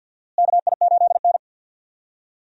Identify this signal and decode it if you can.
Morse code, heard as a keyed tone, about 51 wpm, 700 Hz. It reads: KI8N
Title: USA Call Signs - 4 Characters - Repeated 3x - 50wpm